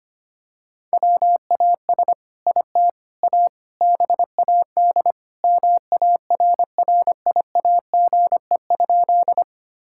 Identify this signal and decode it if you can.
Morse code, heard as a keyed tone, about 25 wpm, 700 Hz. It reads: WAS IT A BAD MARRIAGE?
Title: Sentences from Top 1000 Words - Review Part 5 - 25wpm